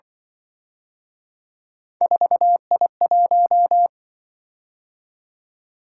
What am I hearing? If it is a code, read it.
Morse code, heard as a keyed tone, about 24 wpm, 700 Hz. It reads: 4I1
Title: Speed-Racing - Three Letter-Number Combinations - 20wpm